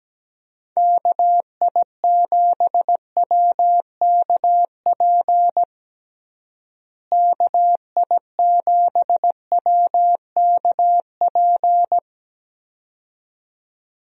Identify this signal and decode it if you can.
Morse code, heard as a keyed tone, about 17 wpm, 700 Hz. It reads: KI7WKP KI7WKP
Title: USA Call Signs - 6 Characters - Repeated 2x - 17wpm